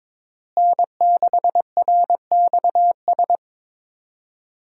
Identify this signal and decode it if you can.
Morse code, heard as a keyed tone, about 22 wpm, 700 Hz. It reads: N6RXS
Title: USA Call Signs - 5 Characters - Repeated 3x - 22wpm